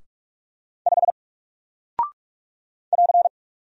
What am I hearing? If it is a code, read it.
Morse code, heard as a keyed tone, about 44 wpm, 700 Hz. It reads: 5 <AR>
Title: Single Character (Letter:Number:Punctuation:Prosign) Alt 45wpm